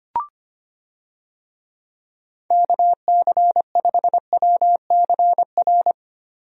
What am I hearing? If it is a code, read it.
Morse code, heard as a keyed tone, about 25 wpm, 700 Hz. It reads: KC5WCR
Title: USA Call Signs - 6 Characters - Repeated 2x - 25wpm